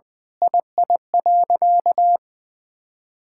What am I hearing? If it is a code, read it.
Morse code, heard as a keyed tone, about 20 wpm, 700 Hz. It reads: II.
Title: Sentences with 7-Word Phrases - 20wpm